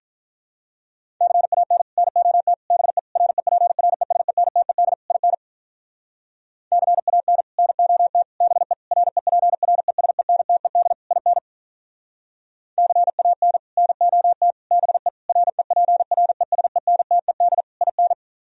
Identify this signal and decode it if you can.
Morse code, heard as a keyed tone, about 46 wpm, 700 Hz. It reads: CAN NOT BE REPRESENTED IN CAN NOT BE REPRESENTED IN CAN NOT BE REPRESENTED IN